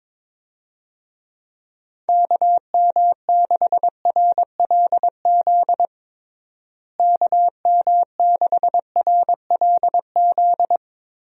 Morse code, heard as a keyed tone, about 22 wpm, 700 Hz. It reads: KM6RLZ KM6RLZ
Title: USA Call Signs - 6 Characters - Repeated 3x - 22wpm